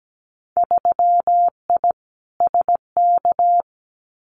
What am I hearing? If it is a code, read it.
Morse code, heard as a keyed tone, about 17 wpm, 700 Hz. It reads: 3I SK
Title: Winter Field Day Exchanges Encore 17wpm